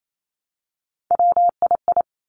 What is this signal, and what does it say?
Morse code, heard as a keyed tone, about 28 wpm, 700 Hz. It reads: WII
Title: Three Challenging Letter-Number Combinations 28wpm